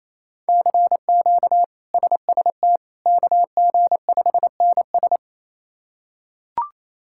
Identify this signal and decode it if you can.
Morse code, heard as a keyed tone, about 28 wpm, 700 Hz. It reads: CQ SST KG5NS
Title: K1USN SST QSO Practice - 28wpm